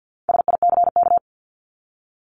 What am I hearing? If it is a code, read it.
Morse code, heard as a keyed tone, about 50 wpm, 700 Hz. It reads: SICK
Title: Top 1501-1750 Words 50wpm